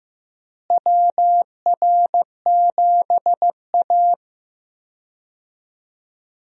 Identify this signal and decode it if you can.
Morse code, heard as a keyed tone, about 15 wpm, 700 Hz. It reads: WR7A
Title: USA Call Signs - 4 Characters - Repeated 2x - 15wpm